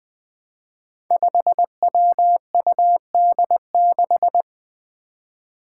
Morse code, heard as a keyed tone, about 20 wpm, 700 Hz. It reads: 5WUD6
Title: Five Challenging Letter-Number Combinations 20wpm